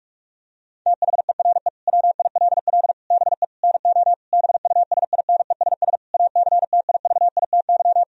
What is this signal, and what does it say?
Morse code, heard as a keyed tone, about 41 wpm, 700 Hz. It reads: THERE WILL BE NO BUSINESS ACTIVITY
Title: Sentences from Top 1000 Words - Review Part 4 - 45wpm